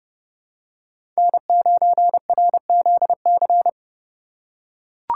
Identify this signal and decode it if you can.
Morse code, heard as a keyed tone, about 32 wpm, 700 Hz. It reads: N9RZC
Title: USA Call Signs - 5 Characters - 3x Character Spacing - 30wpm